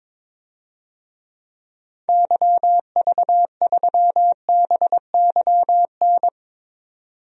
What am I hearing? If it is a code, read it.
Morse code, heard as a keyed tone, about 22 wpm, 700 Hz. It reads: YV3BYN
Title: DXCC-100 Call Signs - 6 Characters - 22wpm